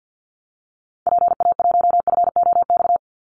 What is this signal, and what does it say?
Morse code, heard as a keyed tone, about 50 wpm, 700 Hz. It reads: PA1FOX
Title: Call Signs Repeated 2x - 50wpm